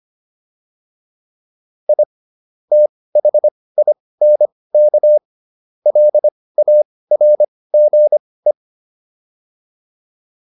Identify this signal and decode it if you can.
Morse code, heard as a keyed tone, about 25 wpm, spaced as for 18 wpm, 600 Hz. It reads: I THINK LARGE